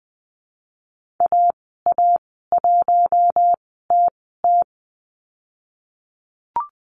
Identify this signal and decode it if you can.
Morse code, heard as a keyed tone, about 19 wpm, spaced as for 14 wpm, 700 Hz. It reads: AA1TT
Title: 055 Lesson 10g - Letter r - Call Signs 20wpm